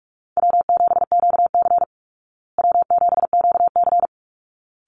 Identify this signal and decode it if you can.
Morse code, heard as a keyed tone, about 44 wpm, 700 Hz. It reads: W7QC W7QC
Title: Call Signs Repeated 3x - 45wpm